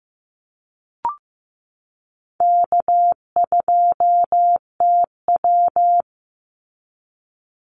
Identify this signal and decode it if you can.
Morse code, heard as a keyed tone, about 15 wpm, 700 Hz. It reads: K2TW
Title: Easy Call Signs Repeated 3x - 15wpm